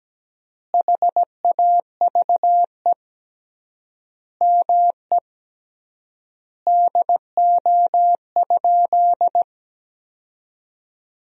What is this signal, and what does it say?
Morse code, heard as a keyed tone, about 17 wpm, 700 Hz. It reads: HAVE ME DO?
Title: 3x Word-Spacing - Sentences from Top 200 Words - 17wpm